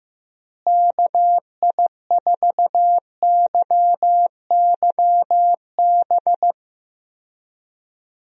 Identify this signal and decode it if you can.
Morse code, heard as a keyed tone, about 15 wpm, 700 Hz. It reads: KI4YYB
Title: USA Call Signs - 6 Characters - Repeated 3x - 15wpm